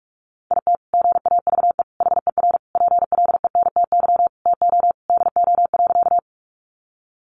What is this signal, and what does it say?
Morse code, heard as a keyed tone, about 44 wpm, 700 Hz. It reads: IT GAVE HER PLENTY TO DO.
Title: Sentences from Top 1300 Words 45wpm